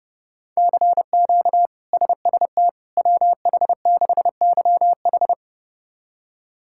Morse code, heard as a keyed tone, about 32 wpm, 700 Hz. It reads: CQ SST WH6YH
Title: K1USN SST QSO Practice - 30wpm